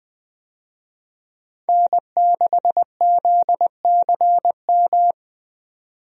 Morse code, heard as a keyed tone, about 20 wpm, 700 Hz. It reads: N6ZCM
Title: USA Call Signs - 5 Characters - 20wpm